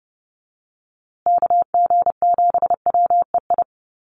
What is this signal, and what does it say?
Morse code, heard as a keyed tone, about 30 wpm, 700 Hz. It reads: KG7WEI
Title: USA Call Signs Repeated 2x 30wpm